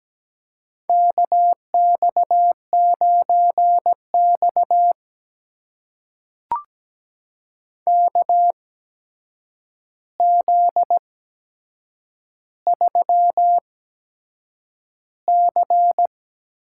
Morse code, heard as a keyed tone, about 17 wpm, spaced as for 7 wpm, 700 Hz. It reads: KX9X KZ3C
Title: USA Call Signs - 4 Characters - 8x Character Spacing - 17wpm